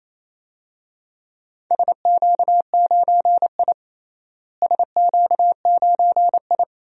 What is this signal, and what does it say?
Morse code, heard as a keyed tone, about 28 wpm, 700 Hz. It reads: SQ9I SQ9I
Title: DXCC-100 Call Signs - 4 Characters - Repeated 3x - 28wpm